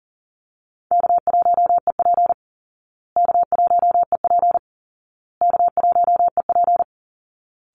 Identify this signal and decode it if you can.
Morse code, heard as a keyed tone, about 40 wpm, 700 Hz. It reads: K1EP K1EP K1EP
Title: Easy Call Signs Repeated 3x - 40wpm